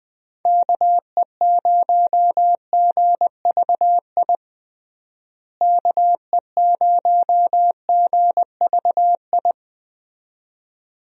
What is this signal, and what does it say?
Morse code, heard as a keyed tone, about 20 wpm, 700 Hz. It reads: KE0GVI KE0GVI
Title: USA Call Signs - 6 Characters - Repeated 2x - 20wpm